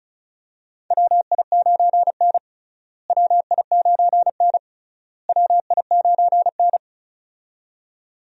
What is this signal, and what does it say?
Morse code, heard as a keyed tone, about 36 wpm, 700 Hz. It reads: WI9N WI9N WI9N
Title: USA Call Signs - 4 Characters - Repeated 3x - 35wpm